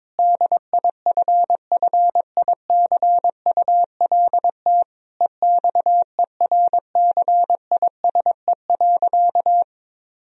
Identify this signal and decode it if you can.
Morse code, heard as a keyed tone, about 22 wpm, 700 Hz. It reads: DIFFICULT EXERCISE.
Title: Sentences from Top 1000 Words - Review Part 4 - 22wpm